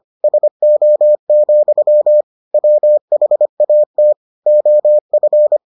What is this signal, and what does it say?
Morse code, heard as a keyed tone, about 25 wpm, 600 Hz. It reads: SO, WHAT OF